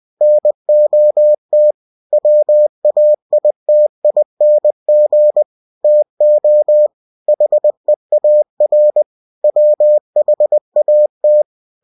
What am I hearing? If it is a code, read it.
Morse code, heard as a keyed tone, about 20 wpm, 600 Hz. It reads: NOT WAITING TO HEAR WHAT
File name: cw-books / winnie-the-pooh / winnie-20wpm-page-0068